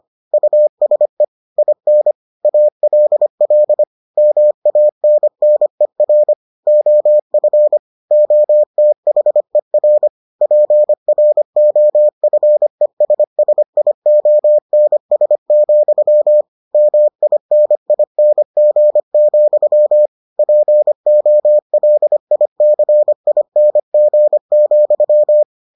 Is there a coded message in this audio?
Yes: Morse code from 0.3 to 25.4 s, heard as a keyed tone, about 25 wpm, 600 Hz. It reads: USE IN ALL MANNER OF OTHER PROFESSIONS, MINING, POLICING,